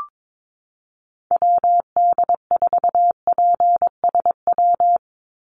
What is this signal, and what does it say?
Morse code, heard as a keyed tone, about 22 wpm, 700 Hz. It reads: WD4PSW
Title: USA Call Signs 22wpm